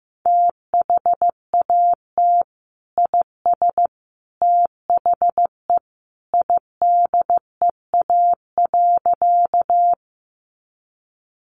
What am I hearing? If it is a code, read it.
Morse code, heard as a keyed tone, about 15 wpm, 700 Hz. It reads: THAT IS THE IDEA.
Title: Sentences from Top 300 Words 15wpm